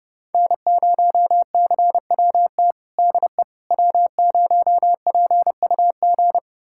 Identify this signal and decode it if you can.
Morse code, heard as a keyed tone, about 30 wpm, 700 Hz. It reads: N0CWT DE W0PUG
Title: USA Callsign DE Callsign - 5 Characters - 30wpm